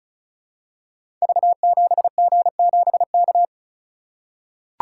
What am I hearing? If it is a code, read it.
Morse code, heard as a keyed tone, about 36 wpm, 700 Hz. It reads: V7G7K
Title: Code Groups - 5 Letter:Numbers - 35wpm